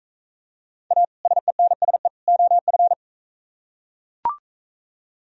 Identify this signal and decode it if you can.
Morse code, heard as a keyed tone, about 41 wpm, 700 Hz. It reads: A SENSE OF